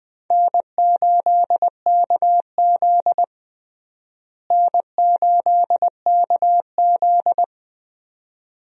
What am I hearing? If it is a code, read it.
Morse code, heard as a keyed tone, about 20 wpm, 700 Hz. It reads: N8KZ N8KZ
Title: USA Call Signs - 4 Characters - Repeated 2x - 20wpm